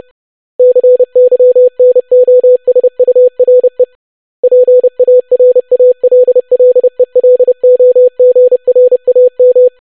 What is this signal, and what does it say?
Morse code, heard as a keyed tone, about 30 wpm, 500 Hz. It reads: CYNOSURE PARALLELOGRAM
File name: longwords-25wpm